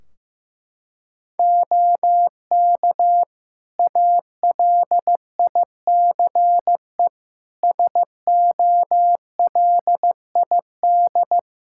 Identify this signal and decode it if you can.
Morse code, heard as a keyed tone, about 15 wpm, 700 Hz. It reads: OK ALICE SOLID
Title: Practice QSO - Exchange 3 - More Info - 15wpm